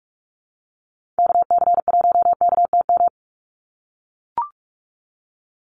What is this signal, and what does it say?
Morse code, heard as a keyed tone, about 44 wpm, 700 Hz. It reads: KC1KTM
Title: USA Call Signs 45wpm